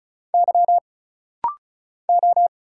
Morse code, heard as a keyed tone, about 36 wpm, 700 Hz. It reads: Y O
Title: Single Letter-Number Alt - Rapid-Fire 35wpm